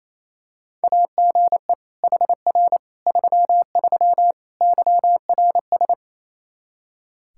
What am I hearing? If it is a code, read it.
Morse code, heard as a keyed tone, about 28 wpm, 700 Hz. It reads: AGE HR 33 YRS
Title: Practice QSO - Exchange 3 - More Info - 28wpm